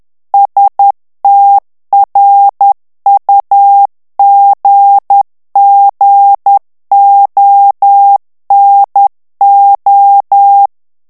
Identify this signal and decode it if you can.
Morse code, heard as a keyed tone, about 10 wpm, 800 Hz. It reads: STRUGGONO
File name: morse mp3 / ITALIEN texte clair 1 vitesse 10 mots minute